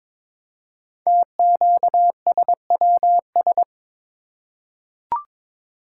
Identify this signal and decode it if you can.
Morse code, heard as a keyed tone, about 22 wpm, 700 Hz. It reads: TQSWS
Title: Code Groups - 5 Letters - 22wpm